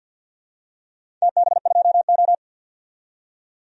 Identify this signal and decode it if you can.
Morse code, heard as a keyed tone, about 52 wpm, 700 Hz. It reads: TB2O
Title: DXCC-100 Call Signs - 4 Characters - 50wpm